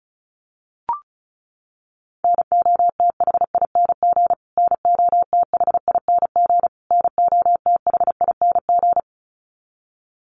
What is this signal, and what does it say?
Morse code, heard as a keyed tone, about 34 wpm, 700 Hz. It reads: NOTHING NOTHING NOTHING
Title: Top 500 Words Repeated 35wpm